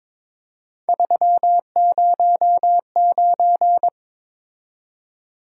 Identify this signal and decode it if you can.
Morse code, heard as a keyed tone, about 26 wpm, 700 Hz. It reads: 309
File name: Speed-Racing - Three Letter-Number Combinations - 22wpm